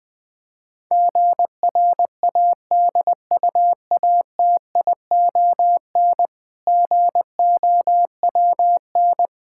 Morse code, heard as a keyed tone, about 20 wpm, 700 Hz. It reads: GRADUATION GOWN
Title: Difficult Words - Speed-Racing with Hints - 20wpm